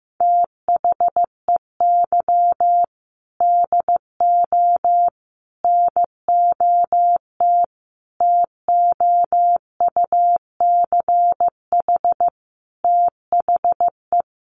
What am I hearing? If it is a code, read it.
Morse code, heard as a keyed tone, about 15 wpm, 700 Hz. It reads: THEY DO NOT TOUCH THE
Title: Sentences from Top 1200 Words 15wpm